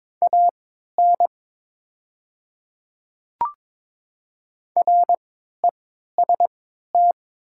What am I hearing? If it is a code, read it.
Morse code, heard as a keyed tone, about 22 wpm, spaced as for 13 wpm, 700 Hz. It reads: AN REST
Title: 3x Character Spacing - Top 500 Words - 22wpm